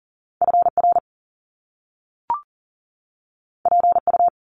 Morse code, heard as a keyed tone, about 42 wpm, 700 Hz. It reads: FR PU
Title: Top Two Letter Combinations in Words 40wpm